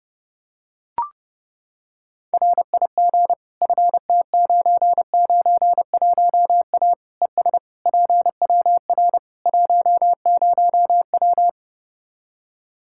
Morse code, heard as a keyed tone, about 32 wpm, 700 Hz. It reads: RIG FT991A ES PWR 10W
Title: K1USN SST QSO Practice 30wpm